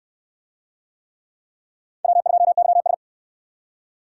Morse code, heard as a keyed tone, about 60 wpm, 700 Hz. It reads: W2YS